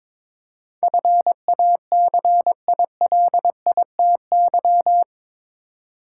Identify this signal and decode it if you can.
Morse code, heard as a keyed tone, about 22 wpm, 700 Hz. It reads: FACILITY
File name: practice / 22wpm / Eight Letter Words - 22wpm